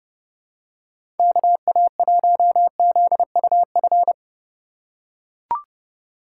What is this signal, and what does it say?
Morse code, heard as a keyed tone, about 30 wpm, 700 Hz. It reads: KA1ZUF
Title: USA Call Signs - 6 Characters - 2x Character Spacing - 30wpm